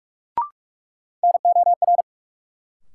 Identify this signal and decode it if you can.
Morse code, heard as a keyed tone, about 46 wpm, 700 Hz. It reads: NOR
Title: Sentences from Top 1500 Words - New Word Drill Alt - 45wpm